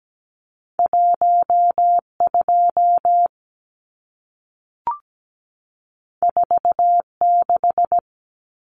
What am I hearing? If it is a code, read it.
Morse code, heard as a keyed tone, about 17 wpm, 700 Hz. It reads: 12 46
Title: Numbers - 2-Digit 17wpm